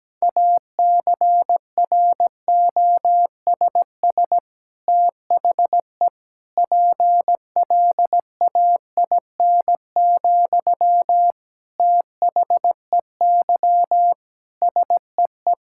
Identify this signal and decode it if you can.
Morse code, heard as a keyed tone, about 17 wpm, 700 Hz. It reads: ACROSS THE PLAIN, THEY SEE
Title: Sentences from Top 1000 Words - Review Part 5 - 17wpm